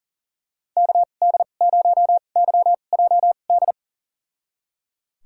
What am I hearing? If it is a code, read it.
Morse code, heard as a keyed tone, about 37 wpm, spaced as for 28 wpm, 700 Hz. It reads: KD0YJD
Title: USA Call Signs - 2x Character Spacing - 40wpm